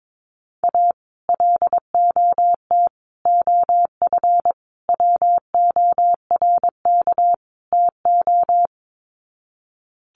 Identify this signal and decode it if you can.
Morse code, heard as a keyed tone, about 22 wpm, 700 Hz. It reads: A LOT OF WORK TO
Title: Sets of 5 Words Encore2 22wpm